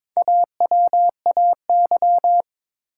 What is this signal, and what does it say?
Morse code, heard as a keyed tone, about 22 wpm, 700 Hz. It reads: AWAY
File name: Build A Phrase - 2-Word Phrases - 22wpm